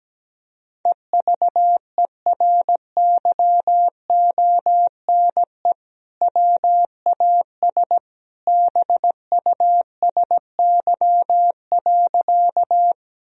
Morse code, heard as a keyed tone, about 17 wpm, 700 Hz. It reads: EVERYONE WAS BUSY.
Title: Sentences from Top 1000 Words - Review Part 3 - 17wpm